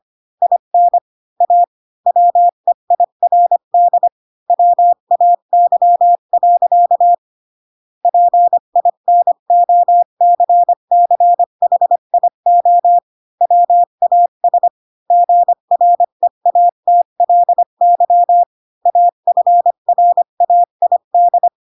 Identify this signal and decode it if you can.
Morse code, heard as a keyed tone, about 25 wpm, 700 Hz. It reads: IN A WEIRD WAY. PINOCCHIO WAS GREATLY AFRAID
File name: Pinocchio / Pinocchio ch6 20 25 ik1hge